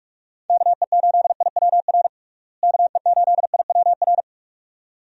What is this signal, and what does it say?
Morse code, heard as a keyed tone, about 46 wpm, 700 Hz. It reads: KE8IWR KE8IWR
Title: USA Call Signs - 6 Characters - Repeated 2x - 45wpm